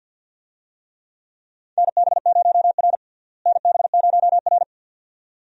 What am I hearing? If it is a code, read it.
Morse code, heard as a keyed tone, about 52 wpm, 700 Hz. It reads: NB0R NB0R
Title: USA Call Signs - 4 Characters - Repeated 2x - 50wpm